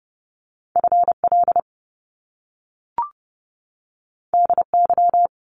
Morse code, heard as a keyed tone, about 30 wpm, 700 Hz. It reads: FL DY
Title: Top Two Letter Combinations in Words 30wpm